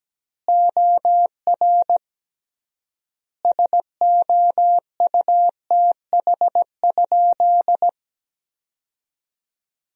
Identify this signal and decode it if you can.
Morse code, heard as a keyed tone, about 17 wpm, 700 Hz. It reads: OR SOUTH?
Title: 3x Word-Spacing - Sentences from Top 400 Words - 17wpm